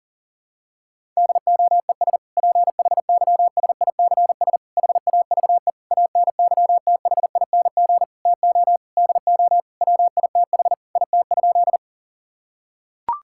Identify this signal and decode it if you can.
Morse code, heard as a keyed tone, about 40 wpm, 700 Hz. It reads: DOES PHYSICS HAVE ANYTHING TO DO WITH IT?
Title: Sentences from Top 1000 Words - Review Part 5 - 40wpm